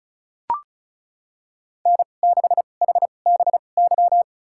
Morse code, heard as a keyed tone, about 36 wpm, spaced as for 25 wpm, 700 Hz. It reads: N6HBY
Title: USA Call Signs - 5 Characters - 2x Character Spacing - 35wpm